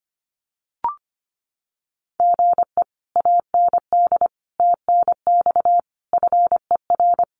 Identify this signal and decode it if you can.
Morse code, heard as a keyed tone, about 25 wpm, 700 Hz. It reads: GE AND TNX FER
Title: Common QSO Phrases Encore 25wpm